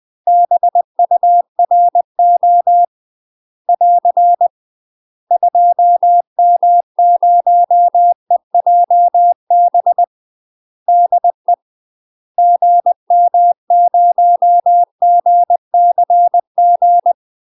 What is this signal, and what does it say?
Morse code, heard as a keyed tone, about 20 wpm, 700 Hz. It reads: BURO <AR> 2M0EJB DE GM0GCG
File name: UK MASTER QSOs TRAINING / 20WPM / UK MASTER QSOs 0018